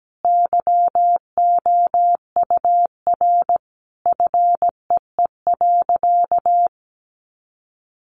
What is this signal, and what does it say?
Morse code, heard as a keyed tone, about 17 wpm, 700 Hz. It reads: YOUR FEE.
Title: A Sound Of Thunder 17wpm